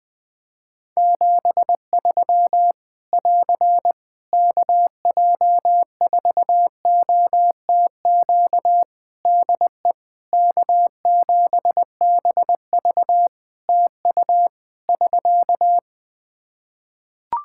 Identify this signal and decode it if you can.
Morse code, heard as a keyed tone, about 21 wpm, 700 Hz. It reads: 73 <AR> KJ4OTQ DE K7BV TU <SK>
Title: K1USN SST QSO Practice 20wpm